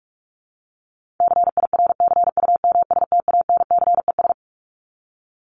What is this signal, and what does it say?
Morse code, heard as a keyed tone, about 44 wpm, 700 Hz. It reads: CIRCUMSTANCES
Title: ES:S Suffix - Top 100 Words with Base Words 45wpm